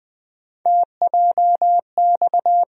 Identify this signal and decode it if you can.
Morse code, heard as a keyed tone, about 20 wpm, 700 Hz. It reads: TJX
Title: USA Call Sign Postfixes - 3 Letters 20wpm